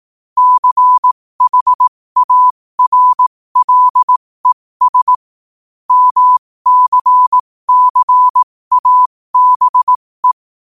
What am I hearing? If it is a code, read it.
Morse code, heard as a keyed tone, about 18 wpm, spaced as for 15 wpm, 1000 Hz. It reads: CHARLES MCCABE
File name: qotdfiles / 15wpm